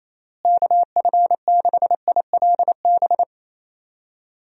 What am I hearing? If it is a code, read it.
Morse code, heard as a keyed tone, about 28 wpm, 700 Hz. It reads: KF6ILB
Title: USA Call Signs - 6 Characters - Repeated 2x - 28wpm